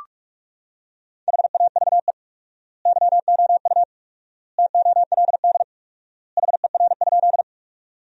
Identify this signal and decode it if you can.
Morse code, heard as a keyed tone, about 46 wpm, 700 Hz. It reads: HAVE YOU TOLD HER?
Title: 4x Word-Spacing - Sentences from Top 400 Words - 45wpm